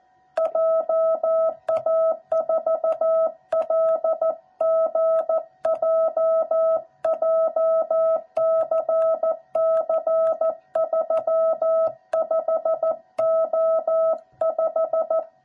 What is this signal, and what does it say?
Morse code, heard as a keyed tone, about 14 wpm, 660 Hz. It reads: JA4LGJ JCC35O5
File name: jh8cbh / 2021cw / cw06b